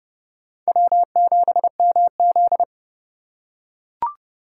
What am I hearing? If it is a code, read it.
Morse code, heard as a keyed tone, about 31 wpm, 700 Hz. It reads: W7MZ